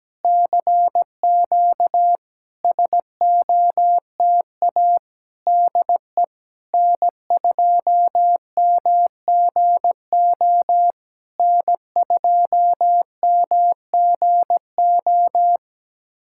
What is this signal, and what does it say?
Morse code, heard as a keyed tone, about 17 wpm, 700 Hz. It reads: CQ SOTA DE N2MGO N2MGO
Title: SOTA QSO Practice 17wpm